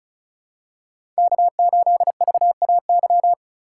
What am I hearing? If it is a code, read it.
Morse code, heard as a keyed tone, about 34 wpm, 700 Hz. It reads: K8VAY
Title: USA Call Signs - 5 Characters - 3x Character Spacing - 35wpm